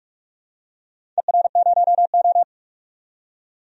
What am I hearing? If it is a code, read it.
Morse code, heard as a keyed tone, about 46 wpm, 700 Hz. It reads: ER0O
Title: DXCC-100 Call Signs - 4 Characters - 45wpm